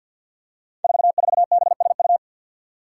Morse code, heard as a keyed tone, about 51 wpm, 700 Hz. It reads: 44BSU
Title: Five Challenging Letter-Number Combinations 50wpm